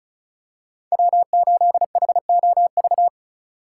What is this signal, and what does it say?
Morse code, heard as a keyed tone, about 36 wpm, 700 Hz. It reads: W8HOV
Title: USA Call Signs - 3x Character Spacing - 35wpm